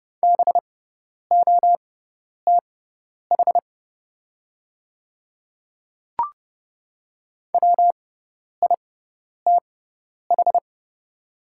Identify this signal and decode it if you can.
Morse code, heard as a keyed tone, about 30 wpm, spaced as for 10 wpm, 700 Hz. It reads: BOTH WITH
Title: 6x Character Spacing - Top 500 Words - 30wpm